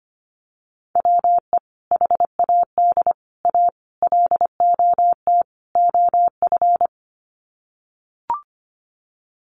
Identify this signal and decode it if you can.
Morse code, heard as a keyed tone, about 25 wpm, 700 Hz. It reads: WE HAD A LOT OF
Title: Sets of 5 Words Encore 25wpm